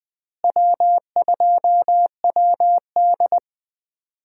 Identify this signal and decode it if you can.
Morse code, heard as a keyed tone, about 20 wpm, 700 Hz. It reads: W2WD
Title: USA Call Signs - 4 Characters - Repeated 2x - 20wpm